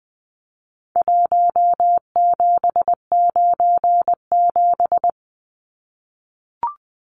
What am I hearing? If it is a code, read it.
Morse code, heard as a keyed tone, about 20 wpm, 700 Hz. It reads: 1797
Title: Numbers - Contest Style - 4-Digit 20wpm